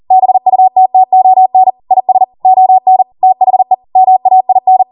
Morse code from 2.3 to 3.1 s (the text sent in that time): ON